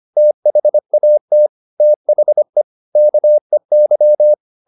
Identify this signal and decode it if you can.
Morse code, heard as a keyed tone, about 25 wpm, 600 Hz. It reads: THAT THE KEY